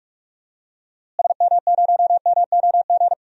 Morse code, heard as a keyed tone, about 46 wpm, 700 Hz. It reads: SM0MOG
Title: DXCC-100 Call Signs - 6 Characters - Repeated 2x - 45wpm